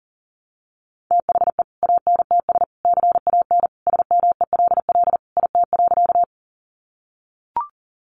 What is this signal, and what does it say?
Morse code, heard as a keyed tone, about 40 wpm, 700 Hz. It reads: THE ANTS CAN SMELL IT.
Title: Additional Easy Phrases 40wpm